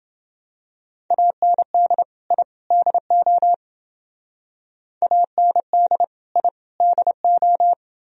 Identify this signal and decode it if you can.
Morse code, heard as a keyed tone, about 28 wpm, 700 Hz. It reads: AND I DO AND I DO